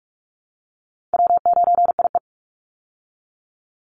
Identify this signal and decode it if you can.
Morse code, heard as a keyed tone, about 44 wpm, 700 Hz. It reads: W9IE
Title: Call Signs Repeated 2x - 45wpm